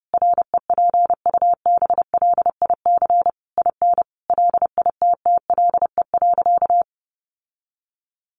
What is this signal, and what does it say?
Morse code, heard as a keyed tone, about 28 wpm, 700 Hz. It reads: REPUBLIC IN LITTLE.
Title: Sentences from Top 1500 Words 30wpm